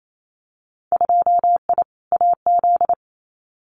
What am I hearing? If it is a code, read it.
Morse code, heard as a keyed tone, about 28 wpm, 700 Hz. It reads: 2I AZ
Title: Winter Field Day Exchanges 28wpm